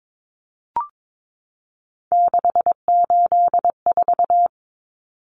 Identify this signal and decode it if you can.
Morse code, heard as a keyed tone, about 22 wpm, 700 Hz. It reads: 684
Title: Numbers - 3-Digit 22wpm